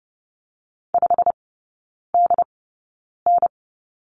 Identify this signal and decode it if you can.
Morse code, heard as a keyed tone, about 30 wpm, spaced as for 9 wpm, 700 Hz. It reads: 5DN
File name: Lesson 15e - Number 5 - Three Characters 30wpm